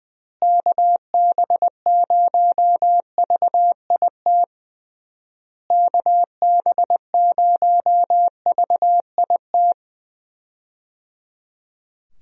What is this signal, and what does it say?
Morse code, heard as a keyed tone, about 20 wpm, 700 Hz. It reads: KB0VIT KB0VIT
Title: USA Call Signs - 6 Characters - Repeated 3x - 20wpm